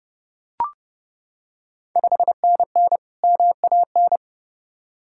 Morse code, heard as a keyed tone, about 30 wpm, 700 Hz. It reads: 5NN MAN
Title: Ontario QSO Party Exchange 30wpm